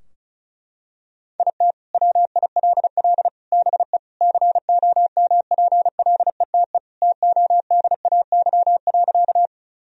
Morse code, heard as a keyed tone, about 34 wpm, 700 Hz. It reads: IT WILL BE COMPLETE TODAY.
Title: Sentences from Top 1000 Words - Review Part 2 - 35wpm